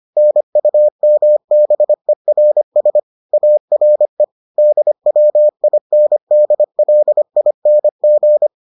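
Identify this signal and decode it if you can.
Morse code, heard as a keyed tone, about 25 wpm, 600 Hz. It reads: NUMBERS ARE DWINDLING
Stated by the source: Morse edition of the Foundations of Amateur Radio episodes.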